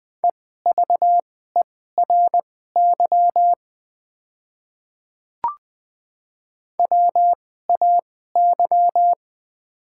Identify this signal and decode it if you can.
Morse code, heard as a keyed tone, about 20 wpm, spaced as for 14 wpm, 700 Hz. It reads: EVERY WAY
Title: 2x Character Spacing - Top 300 Words - 20wpm